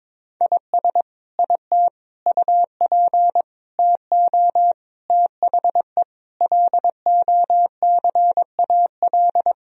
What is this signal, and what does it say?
Morse code, heard as a keyed tone, about 21 wpm, 700 Hz. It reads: IS IT UP TO THE LOCAL
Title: Sentences from Top 1000 Words - Review Part 4 - 22wpm